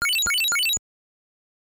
There is no code.